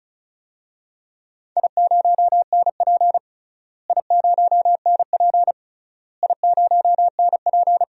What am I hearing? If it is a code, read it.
Morse code, heard as a keyed tone, about 36 wpm, 700 Hz. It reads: I0NP I0NP I0NP
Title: DXCC-100 Call Signs - 4 Characters - Repeated 3x - 35wpm